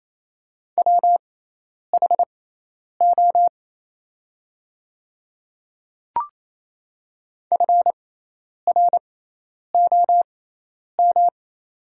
Morse code, heard as a keyed tone, about 28 wpm, spaced as for 10 wpm, 700 Hz. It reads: WHO FROM
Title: 6x Character Spacing - Top 100 Words - 28wpm